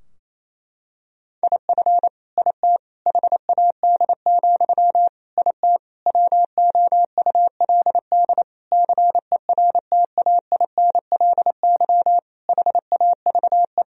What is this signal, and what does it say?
Morse code, heard as a keyed tone, about 28 wpm, 700 Hz. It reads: IF IT HAD, IT WOULD CERTAINLY HAVE
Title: Sentences from Top 1000 Words - Review Part 3 - 28wpm